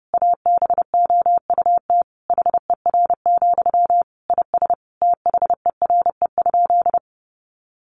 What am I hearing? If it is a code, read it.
Morse code, heard as a keyed tone, about 30 wpm, 700 Hz. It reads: ABOUT HER, IS THERE?
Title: Sentences from Top 1500 Words 30wpm